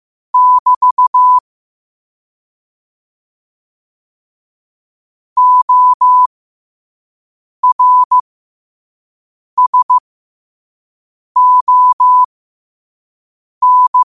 Morse code, heard as a keyed tone, about 15 wpm, spaced as for 5 wpm, 1000 Hz. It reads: <BT> ORSON
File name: qotdfiles / 5wpm